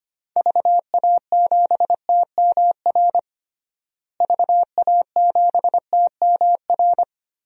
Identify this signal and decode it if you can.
Morse code, heard as a keyed tone, about 25 wpm, 700 Hz. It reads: VA7TMR VA7TMR
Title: DXCC-100 Call Signs - 6 Characters - Repeated 3x - 25wpm